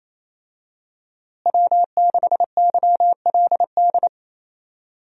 Morse code, heard as a keyed tone, about 28 wpm, 700 Hz. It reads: W6YLD
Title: USA Call Signs - 5 Characters - Repeated 3x - 28wpm